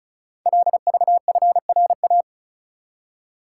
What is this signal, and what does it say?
Morse code, heard as a keyed tone, about 36 wpm, 700 Hz. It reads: LVFRA
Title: Code Groups - 5 Letters - 35wpm